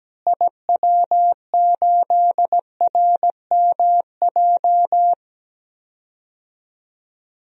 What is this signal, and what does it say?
Morse code, heard as a keyed tone, about 17 wpm, 700 Hz. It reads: IW8RMJ
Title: DXCC-100 Call Signs - 6 Characters - Repeated 2x - 17wpm